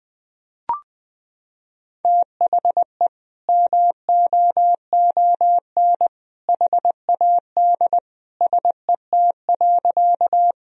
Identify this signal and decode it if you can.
Morse code, heard as a keyed tone, about 21 wpm, 700 Hz. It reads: THE MOON HAD SET.
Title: Sentences from Top 1000 Words - Review Part 3 - 20wpm